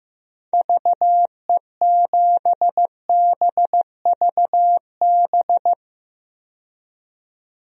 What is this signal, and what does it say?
Morse code, heard as a keyed tone, about 15 wpm, 700 Hz. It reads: VE7BVB
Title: DXCC-100 Call Signs - 6 Characters - 15wpm